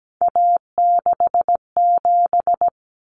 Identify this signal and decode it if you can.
Morse code, heard as a keyed tone, about 17 wpm, 700 Hz. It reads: A67
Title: Common Cut Numbers - 3-Digit 17wpm